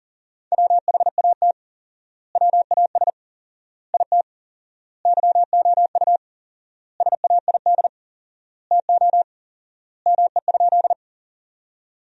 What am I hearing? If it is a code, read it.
Morse code, heard as a keyed tone, about 40 wpm, 700 Hz. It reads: WHAT WAS IT YOU SAID TO ME?
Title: 4x Word-Spacing - Sentences from Top 200 Words - 40wpm